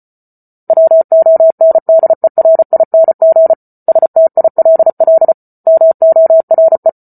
Morse code, heard as a keyed tone, about 34 wpm, 650 Hz. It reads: WONDERING STILL MORE